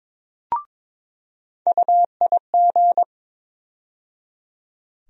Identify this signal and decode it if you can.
Morse code, heard as a keyed tone, about 22 wpm, 700 Hz. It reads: UIG
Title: USA Call Sign Postfixes - 3 Letters 22wpm